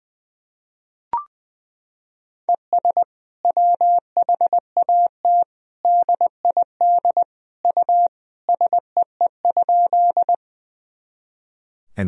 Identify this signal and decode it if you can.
Morse code, heard as a keyed tone, about 20 wpm, 700 Hz. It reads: ES WHAT DID U SEE?
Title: Sentences from Top 100 Words and Abbreviations 20wpm